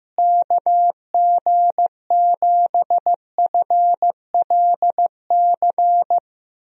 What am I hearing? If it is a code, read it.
Morse code, heard as a keyed tone, about 15 wpm, 700 Hz. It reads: KG7FLC
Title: USA Call Signs - 6 Characters - Repeated 2x - 15wpm